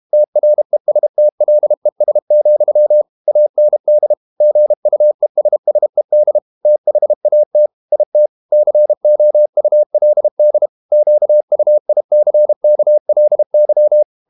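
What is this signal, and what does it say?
Morse code, heard as a keyed tone, about 32 wpm, 600 Hz. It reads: TRESTLES, AND GUESSED THAT IT COULD QUICKLY
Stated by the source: Morse code practice at 32 wpm from CW med Gnister, reading Ken Follett's novel Never in English.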